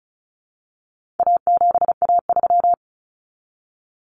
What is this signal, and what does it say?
Morse code, heard as a keyed tone, about 34 wpm, 700 Hz. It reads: A7A3
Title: Common Cut Numbers - 4-Digit 35wpm